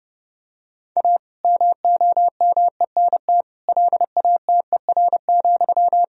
Morse code, heard as a keyed tone, about 30 wpm, 700 Hz. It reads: A MOMENT LATER,
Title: Sentences from Top 1000 Words - Review Part 4 - 30wpm